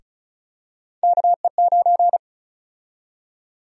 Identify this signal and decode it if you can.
Morse code, heard as a keyed tone, about 36 wpm, 700 Hz. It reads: KE9
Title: USA Call Sign Prefixes - 35wpm